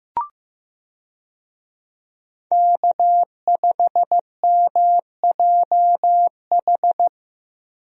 Morse code, heard as a keyed tone, about 15 wpm, 700 Hz. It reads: K5MJH
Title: USA Call Signs - 5 Characters - Repeated 2x - 15wpm